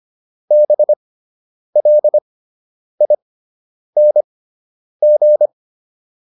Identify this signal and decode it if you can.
Morse code, heard as a keyed tone, about 25 wpm, spaced as for 9 wpm, 600 Hz. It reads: BLING